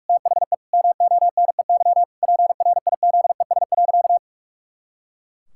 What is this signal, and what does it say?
Morse code, heard as a keyed tone, about 46 wpm, 700 Hz. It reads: THE MONEY PRIZES.
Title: Pangram Challenge v2 - 45wpm